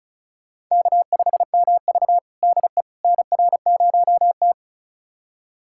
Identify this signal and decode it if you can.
Morse code, heard as a keyed tone, about 36 wpm, 700 Hz. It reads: K5MV DE NR0T
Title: USA Callsign DE Callsign - 4 Characters - 35wpm